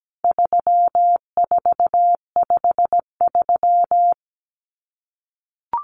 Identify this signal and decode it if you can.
Morse code, heard as a keyed tone, about 17 wpm, 700 Hz. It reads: 3453
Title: Numbers - 4-Digit 17wpm